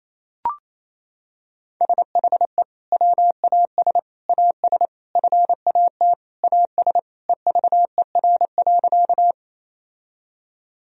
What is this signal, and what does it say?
Morse code, heard as a keyed tone, about 28 wpm, 700 Hz. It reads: SHE WAS AS FAT AS EVER.
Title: Sentences from Top 1000 Words - Review Part 4 - 28wpm